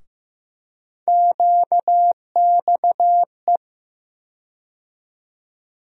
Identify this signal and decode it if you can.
Morse code, heard as a keyed tone, about 15 wpm, 700 Hz. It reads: QXE
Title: USA Call Sign Postfixes - 3 Letters 15wpm